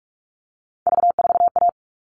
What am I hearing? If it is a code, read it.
Morse code, heard as a keyed tone, about 44 wpm, 700 Hz. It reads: V4A
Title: All Cut Numbers - 3-Digit 45wpm